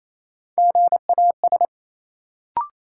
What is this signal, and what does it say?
Morse code, heard as a keyed tone, about 28 wpm, 700 Hz. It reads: GAS
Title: Sentences from Top 1100 Words - New Word Drill - 28wpm